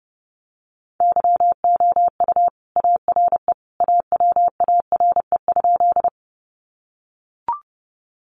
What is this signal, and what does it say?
Morse code, heard as a keyed tone, about 30 wpm, 700 Hz. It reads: YOU ARE AWARE?
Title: Sentences from Top 1200 Words 30wpm